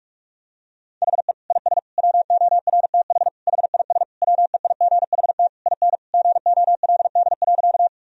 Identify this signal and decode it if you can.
Morse code, heard as a keyed tone, about 47 wpm, 700 Hz. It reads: HE IS WORTH HIS WEIGHT IN GOLD.
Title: Sentences from Top 1000 Words - Review Part 3 - 45wpm